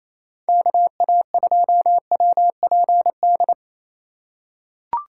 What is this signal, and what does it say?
Morse code, heard as a keyed tone, about 28 wpm, 700 Hz. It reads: KA2WPD
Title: USA Call Signs - 2x Character Spacing - 28wpm